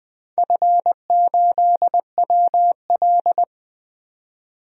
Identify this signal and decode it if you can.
Morse code, heard as a keyed tone, about 20 wpm, 700 Hz. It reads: F8WL
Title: DXCC-100 Call Signs - 4 Characters - 20wpm